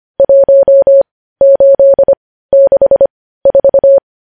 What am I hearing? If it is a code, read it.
Morse code, heard as a keyed tone, about 25 wpm, spaced as for 15 wpm, 550 Hz. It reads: 1864
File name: podcast / News headlines-1x-25x15